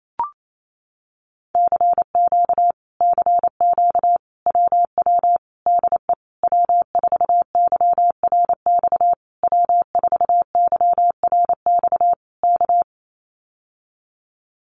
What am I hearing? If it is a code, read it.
Morse code, heard as a keyed tone, about 28 wpm, 700 Hz. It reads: CQ CQ WW DE W4YRX W4YRX K
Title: Common QSO Phrases Encore 28wpm